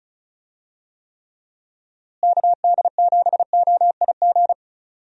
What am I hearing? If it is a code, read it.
Morse code, heard as a keyed tone, about 34 wpm, 700 Hz. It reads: KD7OIG
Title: USA Call Signs - 6 Characters - Repeated 3x - 35wpm